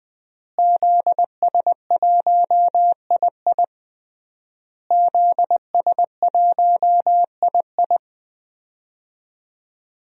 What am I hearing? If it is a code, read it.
Morse code, heard as a keyed tone, about 21 wpm, 700 Hz. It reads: ZS1II ZS1II
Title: DXCC-100 Call Signs - 5 Characters - Repeated 3x - 20wpm